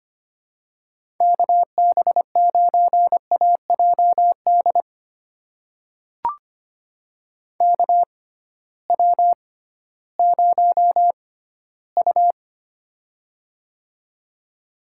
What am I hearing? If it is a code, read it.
Morse code, heard as a keyed tone, about 25 wpm, spaced as for 15 wpm, 700 Hz. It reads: KB9AJD KW0U
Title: USA Call Signs - 6x Character Spacing - 25wpm